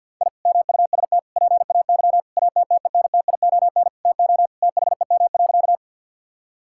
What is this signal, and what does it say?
Morse code, heard as a keyed tone, about 50 wpm, 700 Hz. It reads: I MUST PAY ATTENTION TO THEM.
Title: Sentences from Top 1000 Words - Review Part 6 - 50wpm